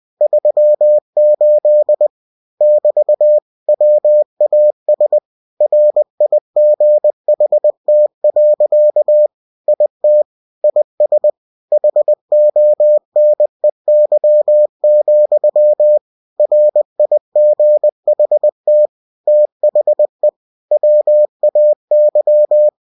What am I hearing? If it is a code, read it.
Morse code, heard as a keyed tone, about 20 wpm, 600 Hz. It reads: 38 <BT> WAS RIGHT. IT IS HONEY, RIGHT THE WAY
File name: cw-books / winnie-the-pooh / winnie-20wpm-page-0038